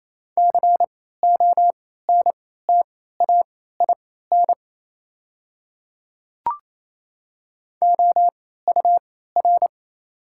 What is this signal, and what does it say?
Morse code, heard as a keyed tone, about 28 wpm, spaced as for 16 wpm, 700 Hz. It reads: CONTAIN OUR
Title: 3x Character Spacing - Top 500 Words - 28wpm